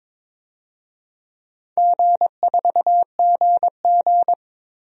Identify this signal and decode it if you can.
Morse code, heard as a keyed tone, about 22 wpm, 700 Hz. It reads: G4GG
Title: DXCC-100 Call Signs - 4 Characters - 22wpm